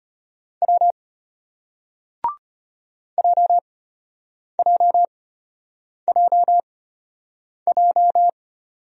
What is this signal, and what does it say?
Morse code, heard as a keyed tone, about 34 wpm, spaced as for 8 wpm, 700 Hz. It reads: W JJJJ